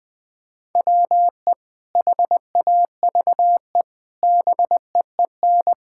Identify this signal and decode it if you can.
Morse code, heard as a keyed tone, about 20 wpm, 700 Hz. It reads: WE HAVE BEEN
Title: Sentences with 4-Word Phrases - 20wpm